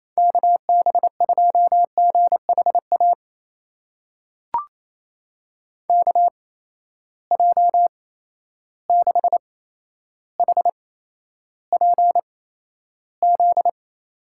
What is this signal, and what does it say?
Morse code, heard as a keyed tone, about 28 wpm, spaced as for 12 wpm, 700 Hz. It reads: KB2GHA KJ6HPZ
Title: USA Call Signs - 6 Characters - 8x Character Spacing - 28wpm